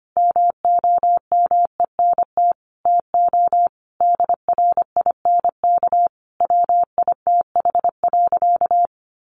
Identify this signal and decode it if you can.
Morse code, heard as a keyed tone, about 25 wpm, 700 Hz. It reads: MOMENT TO DRINK WITH.
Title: Sentences from Top 1500 Words 25wpm